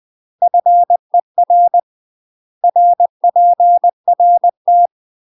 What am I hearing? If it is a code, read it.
Morse code, heard as a keyed tone, about 20 wpm, 700 Hz. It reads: FER RPRT
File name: UK MASTER QSOs TRAINING / 20WPM / UK MASTER QSOs 0047